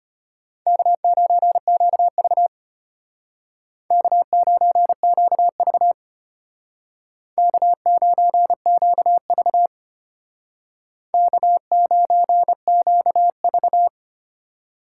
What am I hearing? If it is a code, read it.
Morse code, heard as a keyed tone, about 33 wpm, 700 Hz. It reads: K9QV K9QV K9QV K9QV